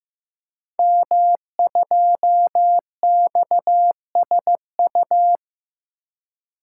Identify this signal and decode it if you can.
Morse code, heard as a keyed tone, about 15 wpm, 700 Hz. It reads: M2XSU
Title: Code Groups - 5 Letter:Numbers - 15wpm